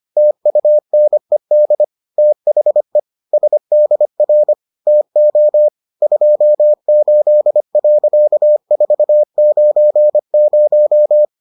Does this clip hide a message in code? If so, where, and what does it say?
Yes: Morse code from 0.2 to 11.2 s, heard as a keyed tone, about 25 wpm, 600 Hz. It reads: TUNED THE SDR TO 28.490